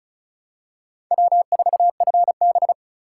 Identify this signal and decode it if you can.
Morse code, heard as a keyed tone, about 34 wpm, 700 Hz. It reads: W4FB
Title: USA Call Signs - 4 Characters - 2x Character Spacing - 35wpm